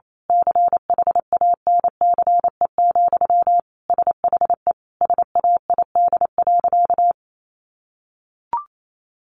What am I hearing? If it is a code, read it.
Morse code, heard as a keyed tone, about 28 wpm, 700 Hz. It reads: CHANCE, SHE SAID.
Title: Sentences from Top 800 Words 28wpm